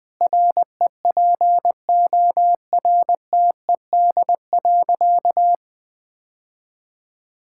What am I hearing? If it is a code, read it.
Morse code, heard as a keyed tone, about 20 wpm, 700 Hz. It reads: REPORTED.
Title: Sentences with 3-Word Phrases - Spoken Beginning - 20wpm